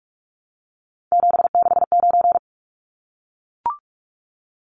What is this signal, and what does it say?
Morse code, heard as a keyed tone, about 44 wpm, 700 Hz. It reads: 769
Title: Numbers - 3-Digit 45wpm